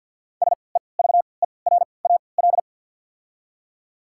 Morse code, heard as a keyed tone, about 48 wpm, spaced as for 28 wpm, 690 Hz. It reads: SEVERAL
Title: 3x Character Spacing - Top 500 Words - 50wpm